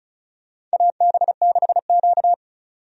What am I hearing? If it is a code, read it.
Morse code, heard as a keyed tone, about 36 wpm, 700 Hz. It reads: AB6Q
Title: USA Call Signs - 4 Characters - 35wpm